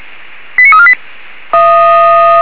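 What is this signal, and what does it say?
Five-tone selective call (ZVEI), heard as quick runs of five tones, tones of about 70 ms. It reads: 89348
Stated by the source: Received by SDR, ZVEI five-tone call 89347.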